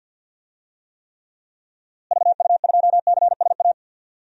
Morse code, heard as a keyed tone, about 51 wpm, 700 Hz. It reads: VU2CSA
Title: DXCC-100 Call Signs - 6 Characters - Repeated 2x - 50wpm